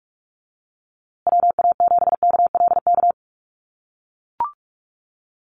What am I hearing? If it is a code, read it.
Morse code, heard as a keyed tone, about 44 wpm, 700 Hz. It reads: WA7KLK
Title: USA Call Signs Repeated 3x 45wpm